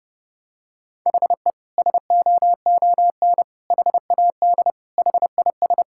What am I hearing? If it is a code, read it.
Morse code, heard as a keyed tone, about 32 wpm, 700 Hz. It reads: HE SOON HAD HIS
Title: Sentences from Top 1600 Words 30wpm